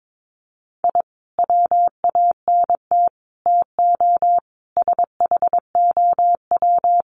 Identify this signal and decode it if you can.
Morse code, heard as a keyed tone, about 22 wpm, 700 Hz. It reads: I WANT TO SHOW
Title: Sets of 5 Words 22wpm